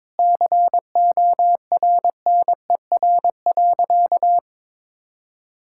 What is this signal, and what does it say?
Morse code, heard as a keyed tone, about 22 wpm, 700 Hz. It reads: CORNER.
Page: morse-code-ninja-practice-22wpm